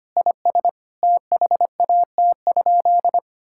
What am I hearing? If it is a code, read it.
Morse code, heard as a keyed tone, about 25 wpm, 700 Hz. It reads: IS THAT?
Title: Sentences from Top 1000 Words - Review Part 5 - 25wpm